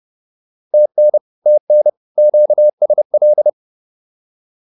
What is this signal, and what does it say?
Morse code, heard as a keyed tone, about 28 wpm, 600 Hz. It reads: TN TN QSL